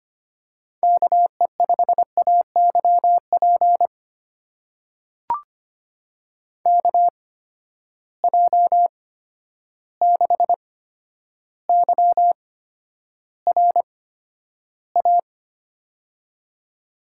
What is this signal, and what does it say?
Morse code, heard as a keyed tone, about 25 wpm, spaced as for 11 wpm, 700 Hz. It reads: KE5AYP KJ6YRA
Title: USA Call Signs - 6 Characters - 8x Character Spacing - 25wpm